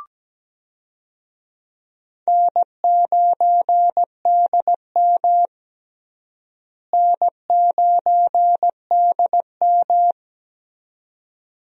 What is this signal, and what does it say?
Morse code, heard as a keyed tone, about 17 wpm, 700 Hz. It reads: N9DM N9DM
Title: USA Call Signs - 4 Characters - Repeated 2x - 17wpm